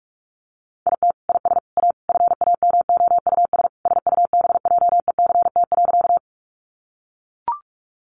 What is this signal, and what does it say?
Morse code, heard as a keyed tone, about 44 wpm, 700 Hz. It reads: IT IS A FAMOUS SUBJECT.
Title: Sentences from Top 1500 Words 45wpm